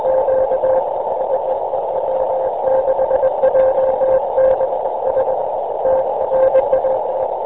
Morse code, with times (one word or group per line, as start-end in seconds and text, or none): none